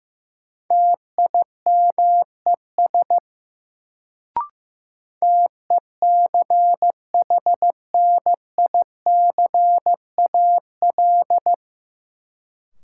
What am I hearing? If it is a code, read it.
Morse code, heard as a keyed tone, about 15 wpm, 700 Hz. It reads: TIMES TECHNICAL
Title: Sentences from Top 1100 Words - New Word Drill Alt - 15wpm